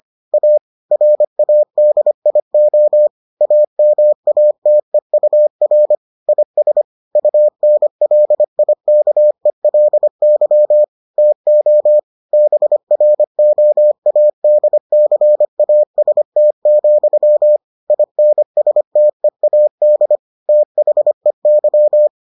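Morse code, heard as a keyed tone, about 25 wpm, 600 Hz. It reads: A RADIO AMATEUR IS UNLIKELY TO BROADCAST, INSTEAD THEY
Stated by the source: Morse edition of the Foundations of Amateur Radio episodes.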